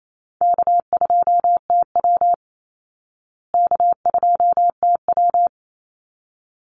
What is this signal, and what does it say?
Morse code, heard as a keyed tone, about 28 wpm, 700 Hz. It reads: K2TW K2TW
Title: Easy Call Signs Repeated 2x - 28wpm